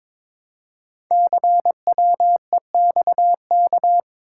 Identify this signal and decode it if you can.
Morse code, heard as a keyed tone, about 22 wpm, 700 Hz. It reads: CWEXK
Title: Code Groups - 5 Letters - 22wpm